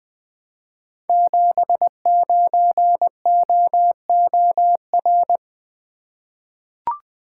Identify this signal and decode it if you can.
Morse code, heard as a keyed tone, about 21 wpm, 700 Hz. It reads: 79OOR
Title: Code Groups - 5 Letter:Numbers - 20wpm